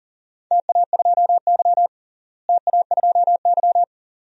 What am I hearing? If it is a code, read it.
Morse code, heard as a keyed tone, about 37 wpm, 700 Hz. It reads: TA2Y TA2Y
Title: DXCC-100 Call Signs - 4 Characters - Repeated 2x - 40wpm